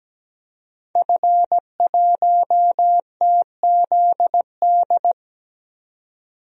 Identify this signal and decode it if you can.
Morse code, heard as a keyed tone, about 17 wpm, 700 Hz. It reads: F1TZD
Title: DXCC-100 Call Signs - 5 Characters - 17wpm